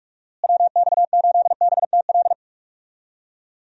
Morse code, heard as a keyed tone, about 42 wpm, 700 Hz. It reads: WX8BTL
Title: USA Call Signs - 6 Characters - Repeated 3x - 45wpm